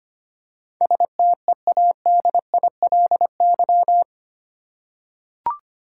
Morse code, heard as a keyed tone, about 25 wpm, 700 Hz. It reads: STEADILY
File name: 8 Letter Words - First 3 Letters Given - 25wpm